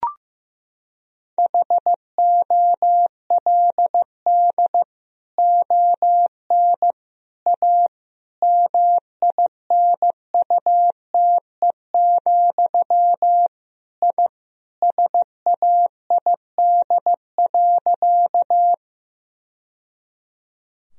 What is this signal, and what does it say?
Morse code, heard as a keyed tone, about 15 wpm, 700 Hz. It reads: HOLD ON A MINUTE, I SAID.
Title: Sentences from Top 1000 Words - Review Part 2 - 15wpm